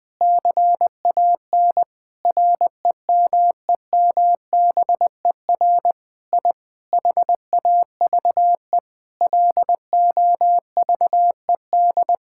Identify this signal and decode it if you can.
Morse code, heard as a keyed tone, about 21 wpm, 700 Hz. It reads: CAN REMEMBER I HAVE LOVED
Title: Sentences with 8-Word Phrases - Spoken Beginning - 20wpm